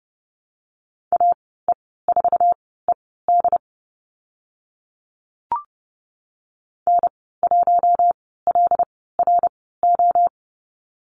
Morse code, heard as a keyed tone, about 30 wpm, spaced as for 17 wpm, 700 Hz. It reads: AE4ED N1LRO
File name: Lesson 14g - Number 2 - Call Signs 30wpm